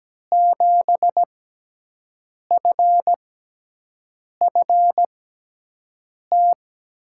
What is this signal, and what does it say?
Morse code, heard as a keyed tone, about 17 wpm, spaced as for 6 wpm, 700 Hz. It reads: 7FFT
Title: USA Call Signs - 6 Characters - 6x Character Spacing - 17wpm